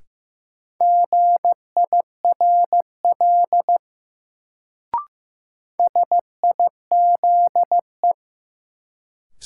Morse code, heard as a keyed tone, about 15 wpm, 700 Hz. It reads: GIRL SIZE
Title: Sentences from Top 400 Words - New Word Drill - 15wpm